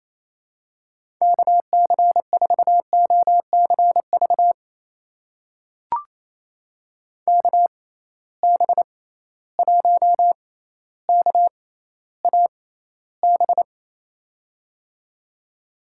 Morse code, heard as a keyed tone, about 28 wpm, spaced as for 14 wpm, 700 Hz. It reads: KC4OCV KB1KAB
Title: USA Call Signs - 6 Characters - 6x Character Spacing - 28wpm